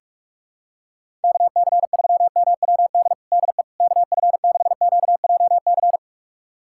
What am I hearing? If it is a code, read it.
Morse code, heard as a keyed tone, about 42 wpm, 700 Hz. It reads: KC3MWD DE KF6QJC
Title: USA Callsign DE Callsign 45wpm